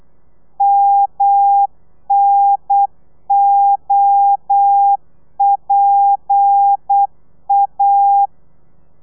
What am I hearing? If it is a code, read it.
Morse code, heard as a keyed tone, about 8 wpm, 800 Hz. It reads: MNOPA
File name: Lez11